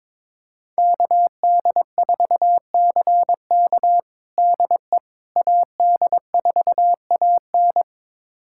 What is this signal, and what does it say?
Morse code, heard as a keyed tone, about 22 wpm, 700 Hz. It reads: KD4CK DE AD4AN
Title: USA Callsign DE Callsign - 5 Characters - 22wpm